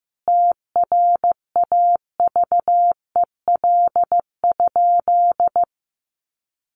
Morse code, heard as a keyed tone, about 15 wpm, 700 Hz. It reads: TRAVEL?
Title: Sentences from Top 1400 Words 15wpm